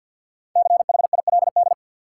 Morse code, heard as a keyed tone, about 52 wpm, 700 Hz. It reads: CHILD
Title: Sentences from Top 800 Words - New Word Drill - 50wpm